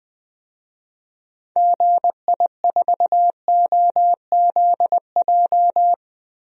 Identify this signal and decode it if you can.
Morse code, heard as a keyed tone, about 20 wpm, 700 Hz. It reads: GI4OZJ
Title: DXCC-100 Call Signs - 6 Characters - 20wpm